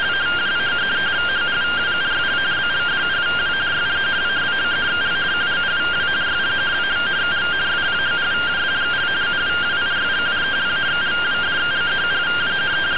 Two-tone teleprinter signal, about 170 Hz shift.